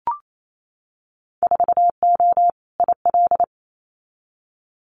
Morse code, heard as a keyed tone, about 28 wpm, 700 Hz. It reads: 4O IL
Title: Winter Field Day Exchanges Encore 28wpm